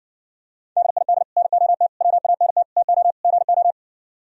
Morse code, heard as a keyed tone, about 62 wpm, 700 Hz. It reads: DID NOT WANT TO GO